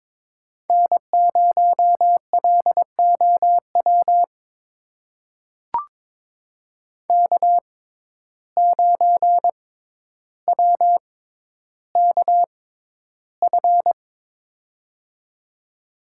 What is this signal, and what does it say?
Morse code, heard as a keyed tone, about 22 wpm, spaced as for 11 wpm, 700 Hz. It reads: N0LOW K9WKF
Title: USA Call Signs - 5 Characters - 6x Character Spacing - 22wpm